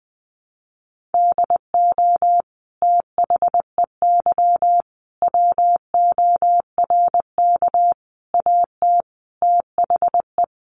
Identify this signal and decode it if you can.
Morse code, heard as a keyed tone, about 20 wpm, 700 Hz. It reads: DO THEY WORK AT THE
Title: Sentences from Top 1300 Words 20wpm